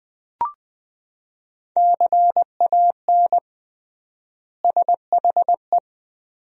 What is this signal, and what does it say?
Morse code, heard as a keyed tone, about 21 wpm, 700 Hz. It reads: CAN SHE
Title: 3x Word-Spacing - Sentences from Top 400 Words - 20wpm